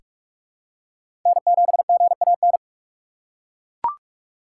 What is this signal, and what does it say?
Morse code, heard as a keyed tone, about 43 wpm, 700 Hz. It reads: N7GAN
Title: USA Call Signs - 5 Characters - 3x Character Spacing - 45wpm